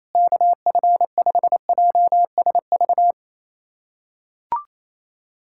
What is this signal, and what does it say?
Morse code, heard as a keyed tone, about 28 wpm, 700 Hz. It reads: KF5JSV
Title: USA Callsign DE Callsign - 6 Characters - 28wpm